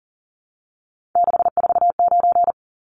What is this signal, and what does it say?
Morse code, heard as a keyed tone, about 40 wpm, 700 Hz. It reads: 649
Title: Numbers - 3-Digit 40wpm